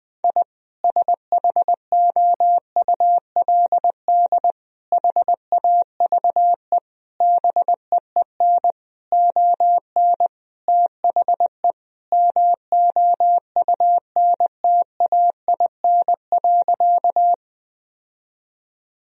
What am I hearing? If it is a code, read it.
Morse code, heard as a keyed tone, about 20 wpm, 700 Hz. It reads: I SHOULD HAVE BEEN ON THE MOUNTAIN.
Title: Sentences from Top 1000 Words - Review Part 1 - 20wpm